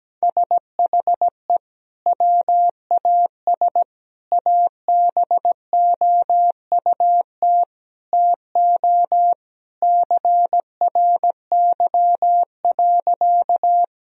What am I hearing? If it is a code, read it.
Morse code, heard as a keyed tone, about 17 wpm, 700 Hz. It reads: SHE WAS ABOUT TO CRY.
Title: Sentences from Top 1000 Words - Review Part 2 - 17wpm